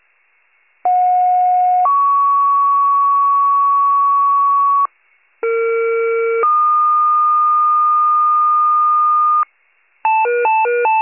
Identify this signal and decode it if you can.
Two-tone page, heard as a one-second tone, then a three-second tone, about 730 then 1120 Hz, about 460 then 1220 Hz. It is heard from 0.8 to 9.4 s.
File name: pager / 21 RESCUE / wcalert